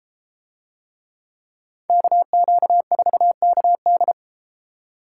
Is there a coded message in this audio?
Yes: Morse code from 1.9 to 4.1 s, heard as a keyed tone, about 32 wpm, 700 Hz. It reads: KQ4KD